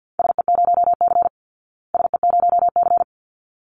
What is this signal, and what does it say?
Morse code, heard as a keyed tone, about 49 wpm, 700 Hz. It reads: SE0C SE0C
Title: Call Signs Repeated 3x - 50wpm